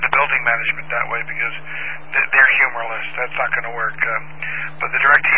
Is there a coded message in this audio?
Yes: Emergency Alert System (SAME) tones from 1.0 to 4.3 s, heard as harsh buzzing bursts, with speech.